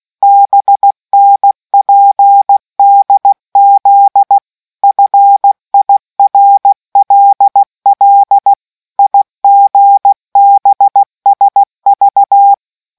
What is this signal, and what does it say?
Morse code, heard as a keyed tone, about 16 wpm, 800 Hz. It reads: BNPDZ FIRLL IGBSV